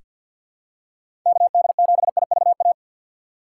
Morse code, heard as a keyed tone, about 47 wpm, 700 Hz. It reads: KD7IVA
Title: USA Call Signs - 2x Character Spacing - 50wpm